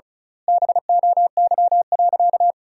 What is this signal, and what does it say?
Morse code, heard as a keyed tone, about 34 wpm, 700 Hz. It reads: BOY.
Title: 2x Word-Spacing - Sentences from Top 200 Words - 35wpm